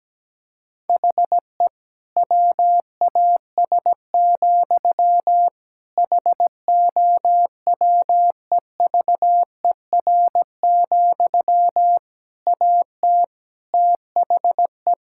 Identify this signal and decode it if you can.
Morse code, heard as a keyed tone, about 17 wpm, 700 Hz. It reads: HE WAS, HOWEVER, AT THE
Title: Sentences from Top 1000 Words - Review Part 5 - 17wpm